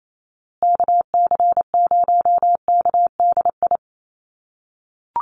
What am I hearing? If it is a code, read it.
Morse code, heard as a keyed tone, about 28 wpm, 700 Hz. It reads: KC0KDI
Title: USA Call Signs Repeated 3x 28wpm